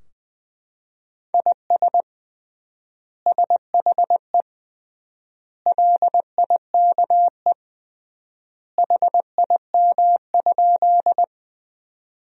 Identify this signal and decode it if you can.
Morse code, heard as a keyed tone, about 20 wpm, 700 Hz. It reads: IS SHE LIKE HIM?
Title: 3x Word-Spacing - Sentences from Top 100 Words - 20wpm